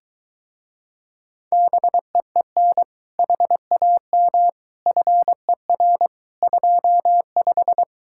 Morse code, heard as a keyed tone, about 23 wpm, 700 Hz. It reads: BEEN HAM FER 25